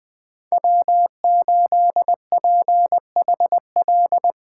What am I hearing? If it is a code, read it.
Morse code, heard as a keyed tone, about 21 wpm, 700 Hz. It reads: W8PHL
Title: USA Call Signs - 5 Characters - 2x Character Spacing - 20wpm